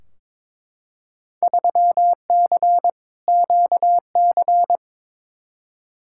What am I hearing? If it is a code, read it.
Morse code, heard as a keyed tone, about 22 wpm, 700 Hz. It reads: 3C QC
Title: Field Day Exchanges Encore - 22wpm